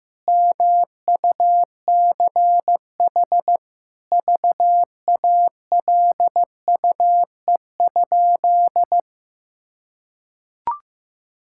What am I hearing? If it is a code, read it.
Morse code, heard as a keyed tone, about 15 wpm, 700 Hz. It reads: MUCH VALUE?
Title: Sentences from Top 1000 Words - Review Part 6 - 15wpm